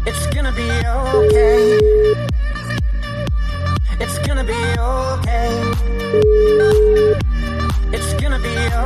Ringback tone (Europe, 425 Hz), with noise.